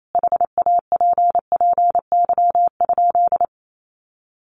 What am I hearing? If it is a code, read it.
Morse code, heard as a keyed tone, about 28 wpm, 700 Hz. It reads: HAPPY?
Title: Sentences from Top 1100 Words 28wpm